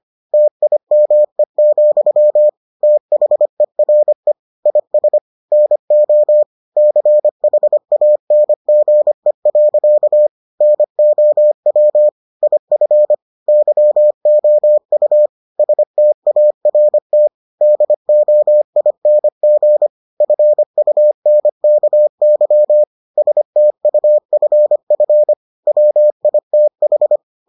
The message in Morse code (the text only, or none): TIME, THERE IS NO CHANGE. NOW IF YOU START DOING FUNKY STUFF WITH